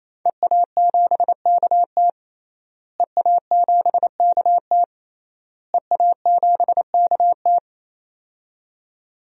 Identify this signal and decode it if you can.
Morse code, heard as a keyed tone, about 28 wpm, 700 Hz. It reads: EA7KT EA7KT EA7KT
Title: DXCC-100 Call Signs - 5 Characters - Repeated 3x - 28wpm